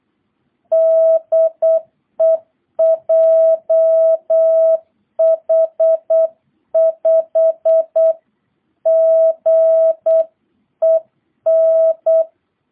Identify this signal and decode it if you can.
Morse code, heard as a keyed tone, about 8 wpm, 660 Hz. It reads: DE JH5GEN